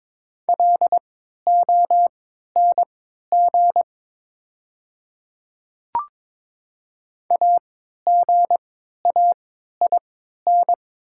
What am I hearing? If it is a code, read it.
Morse code, heard as a keyed tone, about 22 wpm, spaced as for 12 wpm, 690 Hz. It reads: LONG AGAIN
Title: 3x Character Spacing - Top 200 Words - 22wpm